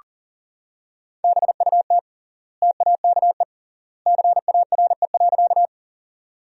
Morse code, heard as a keyed tone, about 37 wpm, 700 Hz. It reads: BUT TAKE CARE.
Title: 3x Word-Spacing - Sentences from Top 300 Words - 40wpm